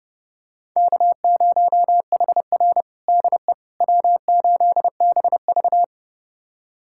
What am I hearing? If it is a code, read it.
Morse code, heard as a keyed tone, about 32 wpm, 700 Hz. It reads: K0HR DE W8BV
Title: USA Callsign DE Callsign - 4 Characters - 30wpm